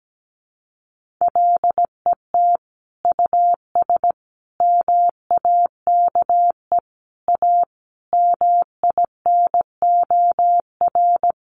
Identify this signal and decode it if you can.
Morse code, heard as a keyed tone, about 17 wpm, 700 Hz. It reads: LET US MAKE A MINOR
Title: Sentences from Top 1500 Words 17wpm